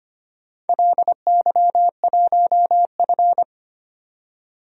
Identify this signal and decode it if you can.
Morse code, heard as a keyed tone, about 25 wpm, 700 Hz. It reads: LY1F
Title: DXCC-100 Call Signs - 4 Characters - 25wpm